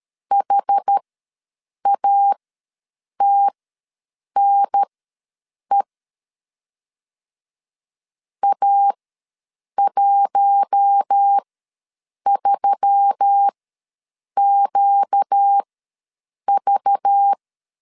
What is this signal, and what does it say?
Morse code, heard as a keyed tone, about 13 wpm, spaced as for 7 wpm, 800 Hz. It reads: HATNE A13QV
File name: LEZIONI / 5 LBVC / 5 LBV ws 12wpm 3015